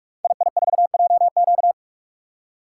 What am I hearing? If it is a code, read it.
Morse code, heard as a keyed tone, about 44 wpm, 700 Hz. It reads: II4JQ
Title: Code Groups - 5 Letter:Numbers - 45wpm